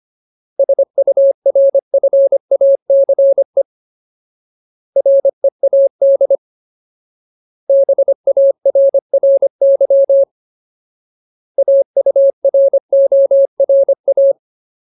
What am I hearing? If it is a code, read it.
Morse code, heard as a keyed tone, about 25 wpm, 550 Hz. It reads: SURFACE READ BARRY AURORA
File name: Ultimate Word List / 25WPM Ultimate